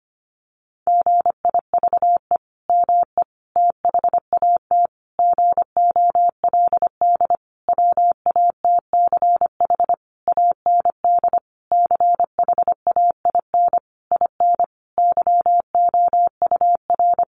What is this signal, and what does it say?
Morse code, heard as a keyed tone, about 25 wpm, 700 Hz. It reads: GIVE ME THAT GOLD WATCH AND CHAIN IN YOUR
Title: Sentences from Top 1500 Words 25wpm